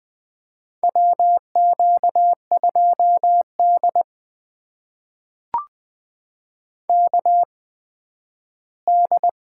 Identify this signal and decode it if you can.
Morse code, heard as a keyed tone, about 20 wpm, spaced as for 12 wpm, 700 Hz. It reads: WQ2D KD
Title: USA Call Signs - 4 Characters - 8x Character Spacing - 20wpm